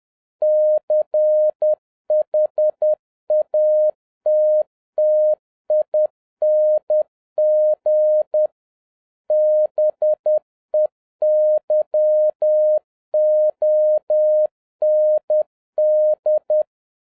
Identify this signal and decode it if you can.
Morse code, heard as a keyed tone, about 10 wpm, 610 Hz. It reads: CHATTING BEYOND